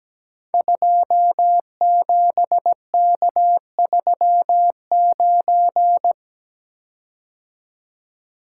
Morse code, heard as a keyed tone, about 17 wpm, 700 Hz. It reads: 27K39
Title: Code Groups - 5 Letter:Numbers - 17wpm